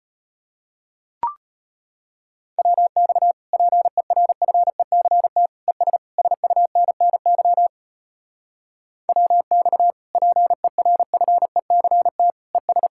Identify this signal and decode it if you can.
Morse code, heard as a keyed tone, about 33 wpm, 700 Hz. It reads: WX PERFECT ES SUNNY WX PERFECT ES